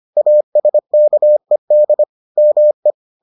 Morse code, heard as a keyed tone, about 25 wpm, 600 Hz. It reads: ASKED ME